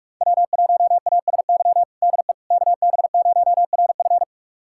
Morse code, heard as a keyed tone, about 41 wpm, 700 Hz. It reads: W1ASY DE KB0RF
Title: USA Callsign DE Callsign - 5 Characters - 45wpm